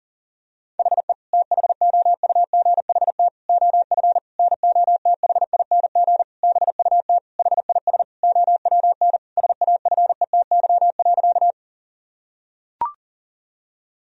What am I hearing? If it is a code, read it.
Morse code, heard as a keyed tone, about 40 wpm, 700 Hz. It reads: HE THOUGHT OF NOTHING BUT HIS OWN SAFETY.
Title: Sentences from Top 1000 Words - Review Part 4 - 40wpm